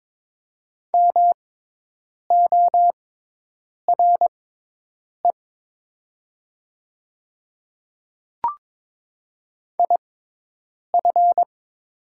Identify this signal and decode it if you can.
Morse code, heard as a keyed tone, about 22 wpm, spaced as for 8 wpm, 700 Hz. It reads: MORE IF
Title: 6x Character Spacing - Top 100 Words - 22wpm